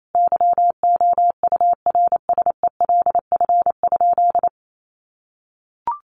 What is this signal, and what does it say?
Morse code, heard as a keyed tone, about 28 wpm, 700 Hz. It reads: YOURSELF?
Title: Sentences from Top 1200 Words 28wpm